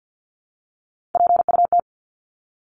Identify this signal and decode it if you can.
Morse code, heard as a keyed tone, about 50 wpm, 700 Hz. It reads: PUT
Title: ING Suffix - Top 100 Words with Base Words 50wpm